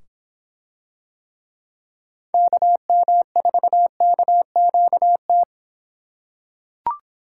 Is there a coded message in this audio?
Yes: Morse code from 2.3 to 5.4 s, heard as a keyed tone, about 25 wpm, 700 Hz. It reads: KM4KQT